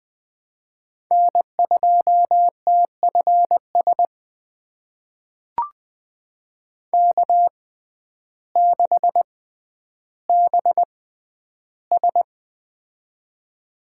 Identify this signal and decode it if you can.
Morse code, heard as a keyed tone, about 21 wpm, spaced as for 11 wpm, 700 Hz. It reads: N2TFS K6BS
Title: USA Call Signs - 6x Character Spacing - 20wpm